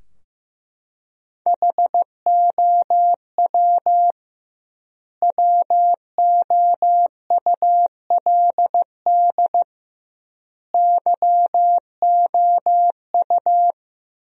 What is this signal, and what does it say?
Morse code, heard as a keyed tone, about 15 wpm, 700 Hz. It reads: HOW WOULD YOU
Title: 2x Word-Spacing - Sentences from Top 100 Words - 15wpm